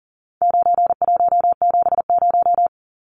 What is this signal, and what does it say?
Morse code, heard as a keyed tone, about 40 wpm, 700 Hz. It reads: 9170
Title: Numbers - 4-Digit 40wpm